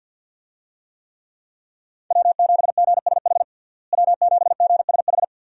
Morse code, heard as a keyed tone, about 46 wpm, 700 Hz. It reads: W7GSH W7GSH
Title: USA Call Signs - 5 Characters - Repeated 2x - 50wpm